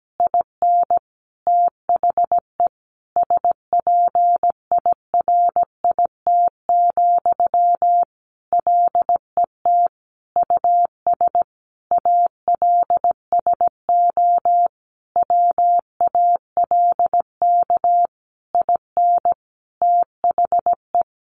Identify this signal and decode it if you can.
Morse code, heard as a keyed tone, about 17 wpm, 700 Hz. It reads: IN THE SPIRIT, LET US ALSO WALK IN THE
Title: Sentences from Top 1400 Words 17wpm